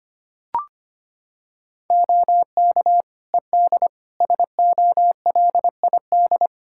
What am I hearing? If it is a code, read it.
Morse code, heard as a keyed tone, about 25 wpm, 700 Hz. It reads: OK ED SOLID
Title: K1USN SST QSO Practice 25wpm